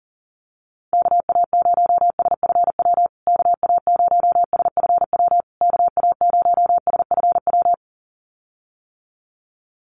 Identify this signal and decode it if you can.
Morse code, heard as a keyed tone, about 40 wpm, 700 Hz. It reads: KA0SFW KA0SFW KA0SFW
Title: USA Call Signs Repeated 3x 40wpm